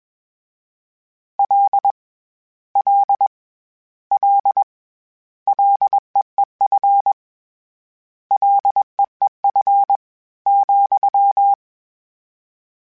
Morse code, heard as a keyed tone, about 21 wpm, 800 Hz. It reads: L L L LEEF LEEF ,
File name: Short Runs / Short Runs-mp3s / SinglePrefixes @20